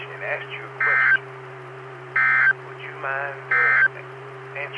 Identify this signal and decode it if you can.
Emergency Alert System (SAME) tones, heard as harsh buzzing bursts, with speech. It is heard from 0.7 to 4.0 s.